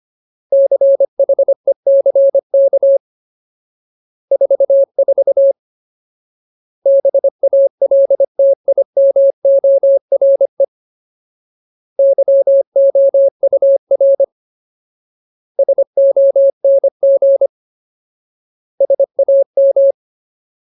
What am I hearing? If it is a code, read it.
Morse code, heard as a keyed tone, about 25 wpm, 550 Hz. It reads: CHECK 44 BALTIMORE YOUR SONG SAM
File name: Ultimate Word List / 25WPM Ultimate